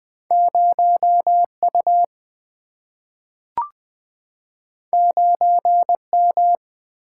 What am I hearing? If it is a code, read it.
Morse code, heard as a keyed tone, about 20 wpm, 700 Hz. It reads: 0 U 9 M
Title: Single-Letter Single-Number 20wpm